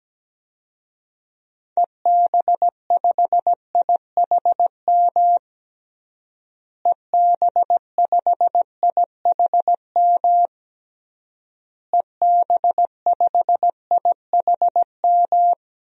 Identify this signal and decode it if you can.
Morse code, heard as a keyed tone, about 17 wpm, 700 Hz. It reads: EB5IHM EB5IHM EB5IHM
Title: DXCC-100 Call Signs - 6 Characters - Repeated 3x - 17wpm